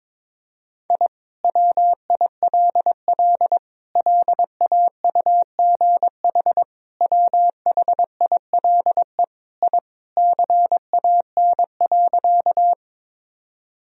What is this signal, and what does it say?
Morse code, heard as a keyed tone, about 22 wpm, 700 Hz. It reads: I WILL LAUGH WHILE I CAN.
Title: Sentences from Top 1000 Words - Review Part 1 - 22wpm